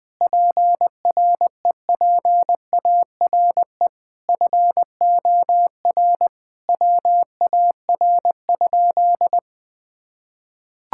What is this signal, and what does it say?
Morse code, heard as a keyed tone, about 20 wpm, 700 Hz. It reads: PREPARE FOR WAR?
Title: Sentences from Top 1600 Words 20wpm